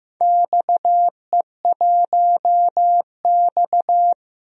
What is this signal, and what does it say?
Morse code, heard as a keyed tone, about 15 wpm, 700 Hz. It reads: XE1X
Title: DXCC-100 Call Signs - 4 Characters - Repeated 3x - 15wpm